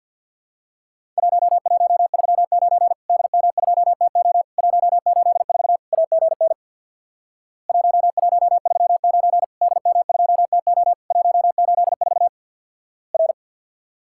Morse code, heard as a keyed tone, about 50 wpm, 700 Hz. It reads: 1139 DM2TO 184 AGN 1139 DM2TO 184 R
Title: Worked All Europe DX Contest - QTC Exchange with Repeats - 50wpm